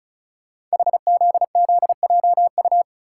Morse code, heard as a keyed tone, about 36 wpm, 700 Hz. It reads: HZZJU
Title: Code Groups - 5 Letters - 35wpm